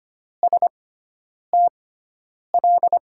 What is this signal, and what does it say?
Morse code, heard as a keyed tone, about 25 wpm, spaced as for 9 wpm, 700 Hz. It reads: STL
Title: USA Call Signs - 5 Characters - 6x Character Spacing - 25wpm